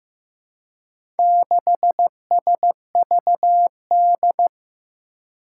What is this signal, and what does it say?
Morse code, heard as a keyed tone, about 15 wpm, 700 Hz. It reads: 6SVD
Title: Four Challenging Letter-Number Combinations 15wpm